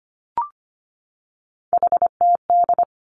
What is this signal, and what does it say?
Morse code, heard as a keyed tone, about 25 wpm, 700 Hz. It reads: HTD
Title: Three Challenging Letter-Number Combinations 25wpm